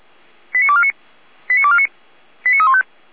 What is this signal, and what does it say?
Five-tone selective call (ZVEI), heard as quick runs of five tones, tones of about 70 ms. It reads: 89249 89249 89315